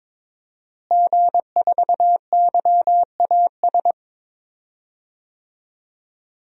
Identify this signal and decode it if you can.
Morse code, heard as a keyed tone, about 22 wpm, 700 Hz. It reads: G4YAS
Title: DXCC-100 Call Signs - 5 Characters - Repeated 2x - 22wpm